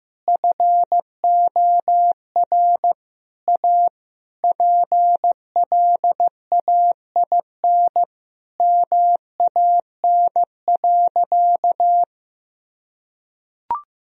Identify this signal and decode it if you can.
Morse code, heard as a keyed tone, about 15 wpm, 700 Hz. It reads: FOR A PLAIN MAN.
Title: Sentences from Top 1000 Words - Review Part 2 - 15wpm